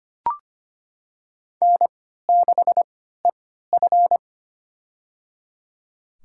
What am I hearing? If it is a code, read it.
Morse code, heard as a keyed tone, about 25 wpm, spaced as for 14 wpm, 700 Hz. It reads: N6EF
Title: USA Call Signs - 4 Characters - 3x Character Spacing - 25wpm